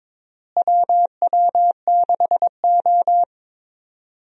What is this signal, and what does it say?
Morse code, heard as a keyed tone, about 21 wpm, 700 Hz. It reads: WW6O
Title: USA Call Signs - 4 Characters - Repeated 3x - 22wpm